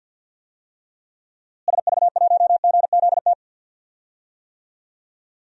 Morse code, heard as a keyed tone, about 48 wpm, 700 Hz. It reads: SV1GZT
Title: DXCC-100 Call Signs - 6 Characters - Repeated 3x - 50wpm